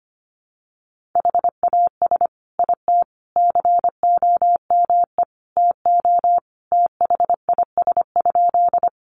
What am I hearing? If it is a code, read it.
Morse code, heard as a keyed tone, about 25 wpm, 700 Hz. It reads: HAS IT COME TO THIS?
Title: Sentences from Top 100 Words 25wpm